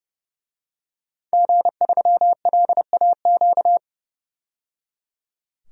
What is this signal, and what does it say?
Morse code, heard as a keyed tone, about 30 wpm, 700 Hz. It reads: G3LAQ
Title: DXCC-100 Call Signs - 5 Characters - 30wpm